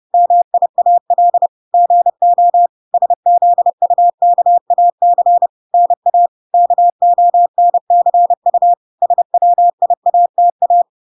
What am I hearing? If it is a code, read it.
Morse code, heard as a keyed tone, about 30 wpm, 700 Hz. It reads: MIAL GO SZUKAC NA KONCU SWIATA